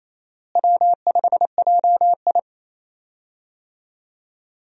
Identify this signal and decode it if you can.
Morse code, heard as a keyed tone, about 27 wpm, 700 Hz. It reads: W5JI
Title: USA Call Signs - 4 Characters - Repeated 3x - 28wpm